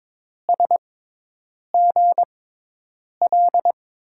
Morse code, heard as a keyed tone, about 22 wpm, spaced as for 8 wpm, 700 Hz. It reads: SGL
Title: USA Call Signs - 6 Characters - 6x Character Spacing - 22wpm